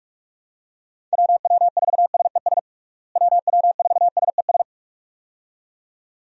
Morse code, heard as a keyed tone, about 42 wpm, 700 Hz. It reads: WW4SES WW4SES
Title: USA Call Signs - 6 Characters - Repeated 2x - 45wpm